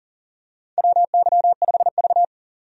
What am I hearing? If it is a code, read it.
Morse code, heard as a keyed tone, about 37 wpm, 700 Hz. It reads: WY5V
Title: USA Call Signs - 4 Characters - 40wpm